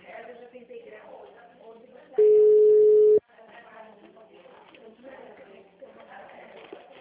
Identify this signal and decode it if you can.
Ringback tone (Europe, 425 Hz), with speech.
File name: out- -2002- - - .12579